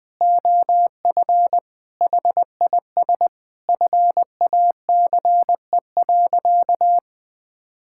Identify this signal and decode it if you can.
Morse code, heard as a keyed tone, about 21 wpm, 700 Hz. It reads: OF HIS FACE.
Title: Sentences from Top 1000 Words - Review Part 5 - 20wpm